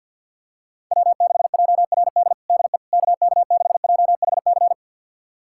Morse code, heard as a keyed tone, about 47 wpm, 700 Hz. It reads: W6JRD DE KK6JHC
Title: USA Callsign DE Callsign 50wpm